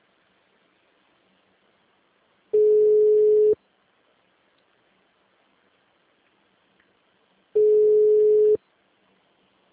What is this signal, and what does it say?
Ringback tone (Europe, 425 Hz).
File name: out- -2009- - - .11910